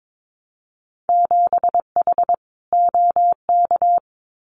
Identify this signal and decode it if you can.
Morse code, heard as a keyed tone, about 22 wpm, 700 Hz. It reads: 7H OK
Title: Winter Field Day Exchanges Encore 22wpm